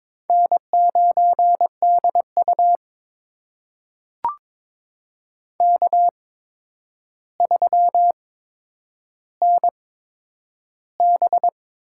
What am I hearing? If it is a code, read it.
Morse code, heard as a keyed tone, about 22 wpm, spaced as for 9 wpm, 700 Hz. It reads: N9DU K3NB
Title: USA Call Signs - 4 Characters - 8x Character Spacing - 22wpm